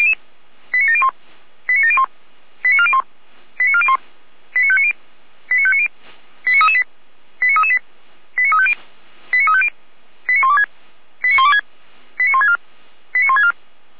Five-tone selective call (ZVEI), heard as quick runs of five tones, tones of about 80 ms. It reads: 89701 89701 89401 89401 89590 89590 89338 89338 89350 89350 89126 89126 89164 89164